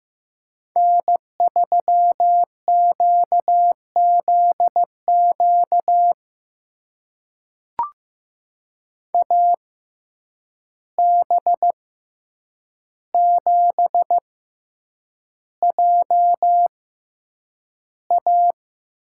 Morse code, heard as a keyed tone, about 15 wpm, spaced as for 8 wpm, 700 Hz. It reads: N3QZQ AB7JA
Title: USA Call Signs - 5 Characters - 6x Character Spacing - 15wpm